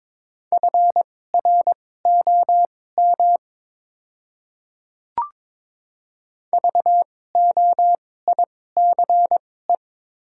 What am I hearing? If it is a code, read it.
Morse code, heard as a keyed tone, about 22 wpm, spaced as for 16 wpm, 700 Hz. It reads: FROM VOICE
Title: 2x Character Spacing - Top 500 Words - 22wpm